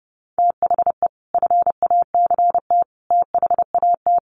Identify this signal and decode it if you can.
Morse code, heard as a keyed tone, about 30 wpm, 700 Hz. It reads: THE FACT THAT
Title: Sets of 5 Words Encore2 30wpm